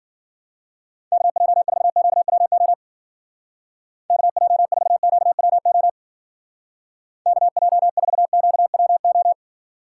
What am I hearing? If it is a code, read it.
Morse code, heard as a keyed tone, about 48 wpm, 690 Hz. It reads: KJ4QWO KJ4QWO KJ4QWO
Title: Speed Racing - USA Call Signs 40wpm